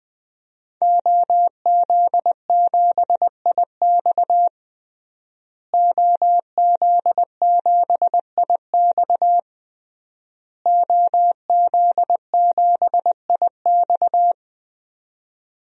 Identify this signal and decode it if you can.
Morse code, heard as a keyed tone, about 20 wpm, 700 Hz. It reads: OZ7IX OZ7IX OZ7IX
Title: DXCC-100 Call Signs - 5 Characters - Repeated 3x - 20wpm